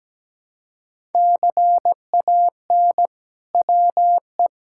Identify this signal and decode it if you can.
Morse code, heard as a keyed tone, about 17 wpm, 700 Hz. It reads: CAN WE
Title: Sentences from Top 1000 Words - Review Part 5 - 17wpm